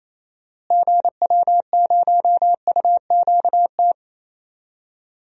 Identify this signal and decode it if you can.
Morse code, heard as a keyed tone, about 28 wpm, 700 Hz. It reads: GW0UQT
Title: DXCC-100 Call Signs - 6 Characters - Repeated 2x - 28wpm